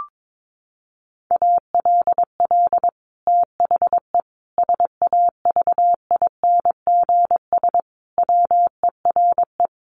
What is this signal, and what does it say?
Morse code, heard as a keyed tone, about 22 wpm, 700 Hz. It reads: ALL THE SAVINGS WERE
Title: Sentences from Top 1200 Words 22wpm